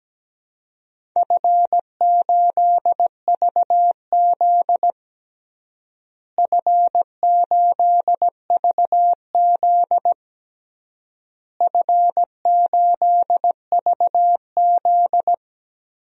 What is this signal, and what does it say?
Morse code, heard as a keyed tone, about 17 wpm, 700 Hz. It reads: F8VZ F8VZ F8VZ
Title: DXCC-100 Call Signs - 4 Characters - Repeated 3x - 17wpm